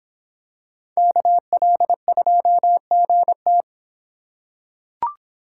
Morse code, heard as a keyed tone, about 25 wpm, 700 Hz. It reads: KL2GT